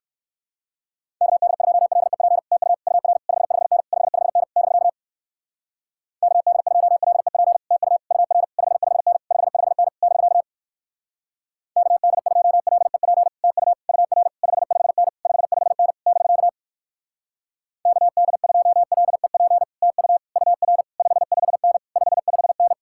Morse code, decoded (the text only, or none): KD2LEP TU UR 55N 55N <BK> KD2LEP TU UR 55N 55N <BK> KD2LEP TU UR 55N 55N <BK> KD2LEP TU UR 55N 55N